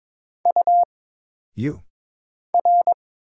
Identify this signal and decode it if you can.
Morse code, heard as a keyed tone, about 22 wpm, 700 Hz. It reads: U R
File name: Single Character (Letter:Number:Punctuation:Prosign) - Mind-Melt - 22wpm